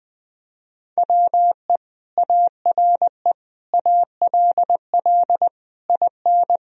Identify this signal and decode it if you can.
Morse code, heard as a keyed tone, about 21 wpm, 700 Hz. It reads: WE ARE ALL IN
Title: Sentences from Top 1000 Words - Review Part 3 - 20wpm